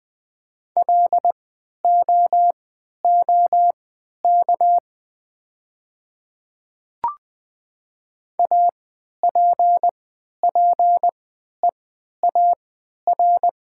Morse code, heard as a keyed tone, about 20 wpm, spaced as for 11 wpm, 700 Hz. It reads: LOOK APPEAR
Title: 3x Character Spacing - Top 500 Words - 20wpm